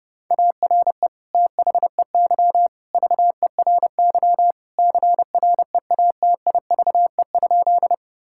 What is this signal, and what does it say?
Morse code, heard as a keyed tone, about 30 wpm, 700 Hz. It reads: ARE THEY VERY CREATIVE?
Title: Sentences from Top 1000 Words - Review Part 5 - 30wpm